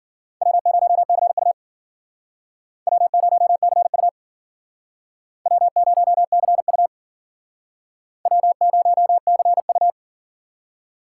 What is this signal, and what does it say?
Morse code, heard as a keyed tone, about 52 wpm, 700 Hz. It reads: W0CU W0CU W0CU W0CU